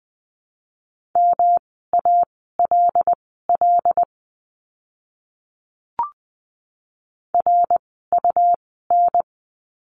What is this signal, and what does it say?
Morse code, heard as a keyed tone, about 20 wpm, spaced as for 15 wpm, 700 Hz. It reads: MALL RUN
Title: 108 Lesson 18f - Letter m - Words 20wpm